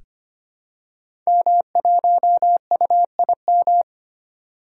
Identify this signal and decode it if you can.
Morse code, heard as a keyed tone, about 25 wpm, 700 Hz. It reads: M1UIM
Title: Code Groups - 5 Letter:Numbers - 25wpm